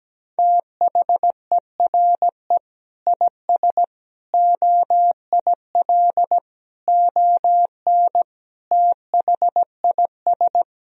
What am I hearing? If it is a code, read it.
Morse code, heard as a keyed tone, about 17 wpm, 700 Hz. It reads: THERE IS OIL ON THIS
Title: Sentences from Top 1000 Words - Review Part 4 - 17wpm